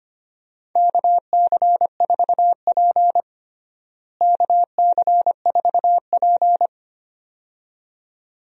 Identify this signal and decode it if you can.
Morse code, heard as a keyed tone, about 25 wpm, 700 Hz. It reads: KC4P KC4P
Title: USA Call Signs - 4 Characters - Repeated 2x - 25wpm